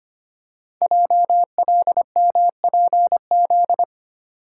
Morse code, heard as a keyed tone, about 25 wpm, 700 Hz. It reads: JLMPZ
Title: Code Groups - 5 Letters - 25wpm